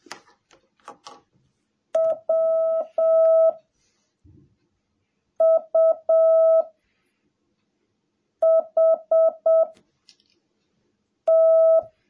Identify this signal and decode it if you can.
Morse code, heard as a keyed tone, about 7 wpm, 650 Hz. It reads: W U H T
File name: jh8cbh / 2021cw / cw17a